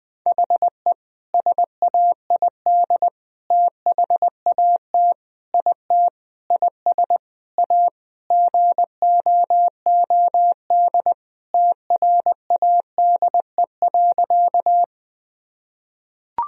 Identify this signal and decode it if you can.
Morse code, heard as a keyed tone, about 21 wpm, 700 Hz. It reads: HE SAID THAT IT IS A GOOD TRADE.
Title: Sentences from Top 1000 Words - Review Part 4 - 20wpm